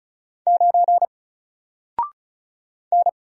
Morse code, heard as a keyed tone, about 36 wpm, 700 Hz. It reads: 9 N
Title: Single Letter-Number Alt 35wpm